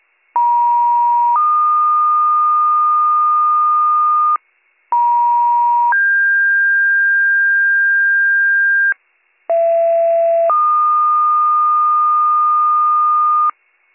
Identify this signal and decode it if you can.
Two-tone page, heard as a one-second tone, then a three-second tone, about 950 then 1220 Hz, about 950 then 1600 Hz, about 670 then 1220 Hz, with noise. It is heard from 0.3 to 13.5 s.